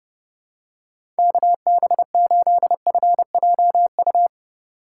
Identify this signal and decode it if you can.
Morse code, heard as a keyed tone, about 30 wpm, 700 Hz. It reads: KB8FJU